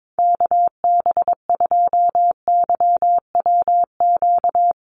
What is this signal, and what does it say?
Morse code, heard as a keyed tone, about 22 wpm, 700 Hz. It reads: KB2YWQ
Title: USA Call Signs Repeated 2x 22wpm